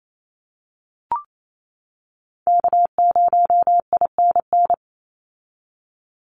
Morse code, heard as a keyed tone, about 28 wpm, 700 Hz. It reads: K0INN
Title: Call Signs - 28wpm